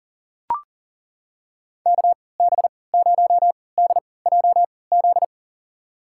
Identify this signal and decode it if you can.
Morse code, heard as a keyed tone, about 39 wpm, spaced as for 23 wpm, 700 Hz. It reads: KB0DJZ
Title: USA Call Signs - 6 Characters - 3x Character Spacing - 40wpm